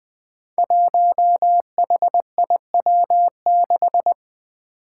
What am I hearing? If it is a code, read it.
Morse code, heard as a keyed tone, about 20 wpm, 700 Hz. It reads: 1HIW6
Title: Five Challenging Letter-Number Combinations 20wpm